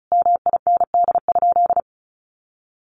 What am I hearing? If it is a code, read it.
Morse code, heard as a keyed tone, about 34 wpm, 700 Hz. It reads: MIND?
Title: Sentences from Top 1400 Words 35wpm